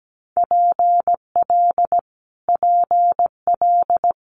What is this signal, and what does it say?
Morse code, heard as a keyed tone, about 17 wpm, 700 Hz. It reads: PL PL
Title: Multi-letter Phonemes Repeated 3x 17wpm